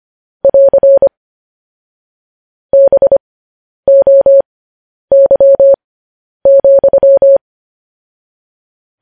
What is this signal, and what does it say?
Morse code, heard as a keyed tone, about 25 wpm, spaced as for 10 wpm, 550 Hz. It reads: <AR> BOY,
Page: news-headlines-in-morse-code-at-10-wpm